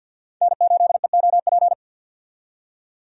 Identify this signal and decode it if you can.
Morse code, heard as a keyed tone, about 52 wpm, 700 Hz. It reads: N8EOP
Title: USA Call Signs - 5 Characters - 50wpm